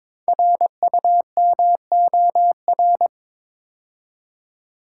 Morse code, heard as a keyed tone, about 22 wpm, 700 Hz. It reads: RUMOR
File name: 5 Letter Words - First 3 Letters Given - 22wpm